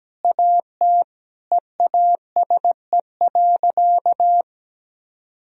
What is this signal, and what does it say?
Morse code, heard as a keyed tone, about 17 wpm, 700 Hz. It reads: AT EASE.
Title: Sentences from Top 1000 Words - Review Part 1 - 17wpm